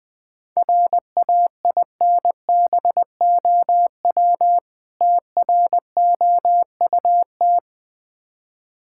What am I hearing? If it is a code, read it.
Morse code, heard as a keyed tone, about 20 wpm, 690 Hz. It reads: RAINBOW TROUT
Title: Difficult Words with Hints - 20wpm